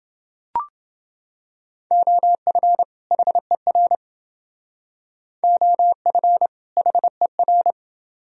Morse code, heard as a keyed tone, about 28 wpm, 700 Hz. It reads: OF HER OF HER